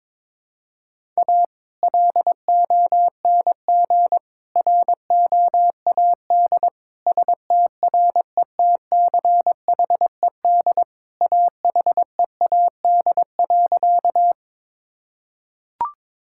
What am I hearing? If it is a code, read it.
Morse code, heard as a keyed tone, about 22 wpm, 700 Hz. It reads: A LONG ROAD STRETCHED AHEAD.
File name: practice / 22wpm / Sentences with 2-Word Phrases - 22wpm